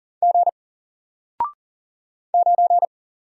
Morse code, heard as a keyed tone, about 38 wpm, 700 Hz. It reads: G 9
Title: Single Character (Letter:Number:Punctuation:Prosign) 40wpm